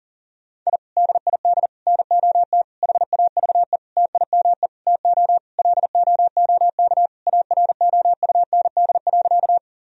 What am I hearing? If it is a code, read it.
Morse code, heard as a keyed tone, about 40 wpm, 700 Hz. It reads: I DID NOT HAVE TIME TO LOOK AROUND.
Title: Sentences from Top 1000 Words - Review Part 3 - 40wpm